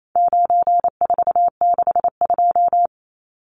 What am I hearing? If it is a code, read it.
Morse code, heard as a keyed tone, about 28 wpm, 700 Hz. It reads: 9462
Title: Numbers - 4-Digit 28wpm